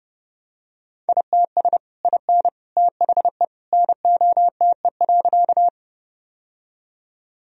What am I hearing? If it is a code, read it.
Morse code, heard as a keyed tone, about 29 wpm, 700 Hz. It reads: ITS IN THE NOTE.
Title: Sentences from Top 1000 Words - Review Part 2 - 30wpm